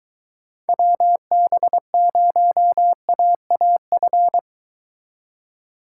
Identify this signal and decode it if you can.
Morse code, heard as a keyed tone, about 22 wpm, 700 Hz. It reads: WB0AAF